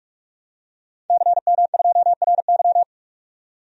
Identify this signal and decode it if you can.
Morse code, heard as a keyed tone, about 44 wpm, 700 Hz. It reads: CM2RY
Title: DXCC-100 Call Signs - 5 Characters - 45wpm